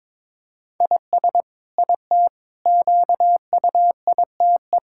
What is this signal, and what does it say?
Morse code, heard as a keyed tone, about 22 wpm, 700 Hz. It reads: IS IT QUITE
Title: Sentences from Top 1000 Words - Review Part 6 - 22wpm